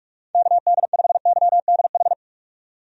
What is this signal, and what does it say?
Morse code, heard as a keyed tone, about 46 wpm, 700 Hz. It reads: KD5YDH
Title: USA Call Signs - 6 Characters - 45wpm